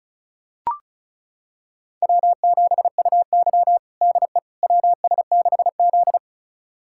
Morse code, heard as a keyed tone, about 34 wpm, 700 Hz. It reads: W7UY DE WS6Z
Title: USA Callsign DE Callsign - 4 Characters - 35wpm